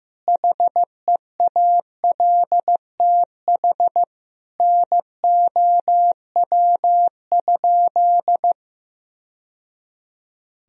Morse code, heard as a keyed tone, about 16 wpm, 700 Hz. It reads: HEALTH NOW?
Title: Sentences from Top 1000 Words - Review Part 3 - 15wpm